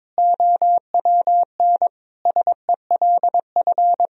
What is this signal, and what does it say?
Morse code, heard as a keyed tone, about 22 wpm, 700 Hz. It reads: OWN SELF
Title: Sentences from Top 1000 Words - Review Part 1 - 22wpm